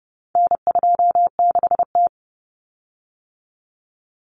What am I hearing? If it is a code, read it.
Morse code, heard as a keyed tone, about 30 wpm, 700 Hz. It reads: N26T
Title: Common Cut Numbers - 4-Digit 30wpm